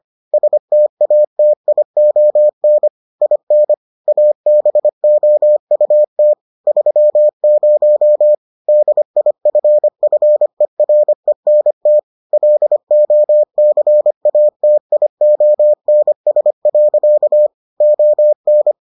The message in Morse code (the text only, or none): STATION IN ABOUT 30 DIFFERENT LOCATIONS. ON